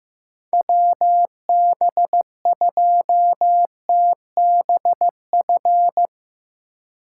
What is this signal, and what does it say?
Morse code, heard as a keyed tone, about 15 wpm, 700 Hz. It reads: WB2TBF
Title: USA Call Signs - 3x Character Spacing - 15wpm